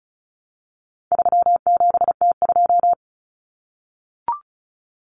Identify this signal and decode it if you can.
Morse code, heard as a keyed tone, about 34 wpm, 700 Hz. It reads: 37T2
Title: Common Cut Numbers - 4-Digit 35wpm